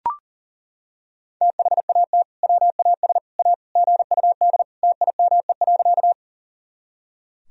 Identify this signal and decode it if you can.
Morse code, heard as a keyed tone, about 40 wpm, 700 Hz. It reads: THAT WAS A GUD TIME.
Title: Sentences from Top 200 Words and Abbreviations 40wpm